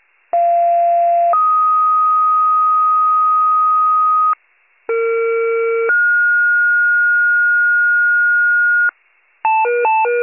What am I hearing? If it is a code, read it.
Two-tone page, heard as a one-second tone, then a three-second tone, about 690 then 1250 Hz, about 460 then 1430 Hz, with noise. It is heard from 0.3 to 8.9 s.